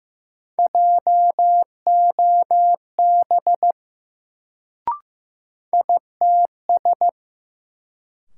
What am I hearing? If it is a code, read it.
Morse code, heard as a keyed tone, about 15 wpm, 700 Hz. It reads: JOB ITS
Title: Sentences from Top 600 Words - New Word Drill - 15wpm